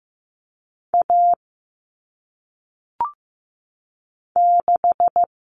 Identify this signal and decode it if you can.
Morse code, heard as a keyed tone, about 15 wpm, 700 Hz. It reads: A 6
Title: Common Cut Numbers - 1-Digit 15wpm